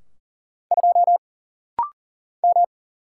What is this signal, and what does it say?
Morse code, heard as a keyed tone, about 40 wpm, 700 Hz. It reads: 2 M
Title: Single Letter-Number Alt - Rapid-Fire 40wpm